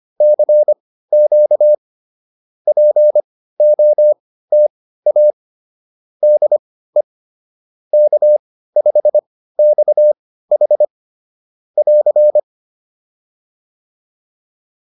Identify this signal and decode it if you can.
Morse code, heard as a keyed tone, about 25 wpm, spaced as for 15 wpm, 590 Hz. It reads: CQ POTA DE K5XH <AR>